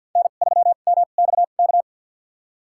Morse code, heard as a keyed tone, about 47 wpm, spaced as for 35 wpm, 700 Hz. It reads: N3MXK
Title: USA Call Signs - 5 Characters - 2x Character Spacing - 50wpm